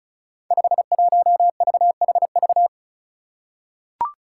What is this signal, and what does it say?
Morse code, heard as a keyed tone, about 34 wpm, 700 Hz. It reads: 51VHV
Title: Five Challenging Letter-Number Combinations 35wpm